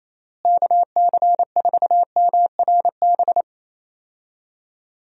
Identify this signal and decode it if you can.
Morse code, heard as a keyed tone, about 28 wpm, 700 Hz. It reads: KC4MRB
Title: USA Call Signs - 6 Characters - Repeated 2x - 28wpm